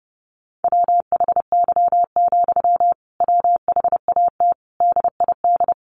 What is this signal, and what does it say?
Morse code, heard as a keyed tone, about 30 wpm, 700 Hz. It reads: WHY, WHAT DID
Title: Sentences from Top 1500 Words 30wpm